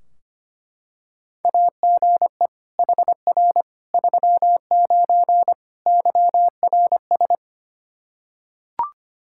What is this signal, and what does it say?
Morse code, heard as a keyed tone, about 25 wpm, 700 Hz. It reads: AGE HR 39 YRS
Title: Practice QSO - Exchange 3 - More Info - 25wpm